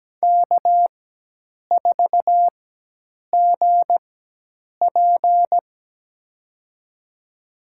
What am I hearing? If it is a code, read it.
Morse code, heard as a keyed tone, about 17 wpm, spaced as for 8 wpm, 700 Hz. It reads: K4GP
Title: USA Call Signs - 4 Characters - 4x Character Spacing - 17wpm